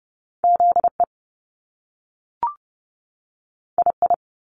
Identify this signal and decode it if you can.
Morse code, heard as a keyed tone, about 30 wpm, 700 Hz. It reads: ZE II
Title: Top Two Letter Combinations in Words 30wpm